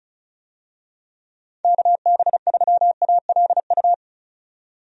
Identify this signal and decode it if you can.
Morse code, heard as a keyed tone, about 34 wpm, 700 Hz. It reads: KB3ALU
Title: USA Call Signs - 6 Characters - 35wpm